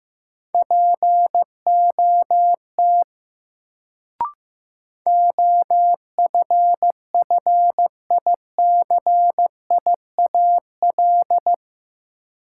Morse code, heard as a keyed tone, about 15 wpm, 700 Hz. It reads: POT OFFICIAL
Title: Sentences from Top 1000 Words - New Word Drill - 15wpm